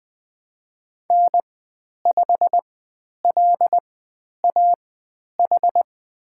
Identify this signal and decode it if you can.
Morse code, heard as a keyed tone, about 20 wpm, spaced as for 10 wpm, 700 Hz. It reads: N5LAH
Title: 102 Lesson 17g - Letter u - Call Signs 20wpm